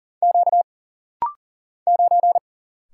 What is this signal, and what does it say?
Morse code, heard as a keyed tone, about 40 wpm, 700 Hz. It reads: Q 9
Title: Single Letter-Number - Rapid-Fire 40wpm